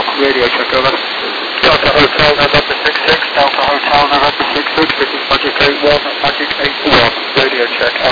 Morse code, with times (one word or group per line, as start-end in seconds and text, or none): none